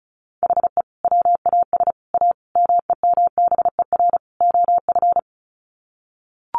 Morse code, heard as a keyed tone, about 34 wpm, 700 Hz. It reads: HE WAS A MEMBER OF
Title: Sets of 5 Words Encore2 35wpm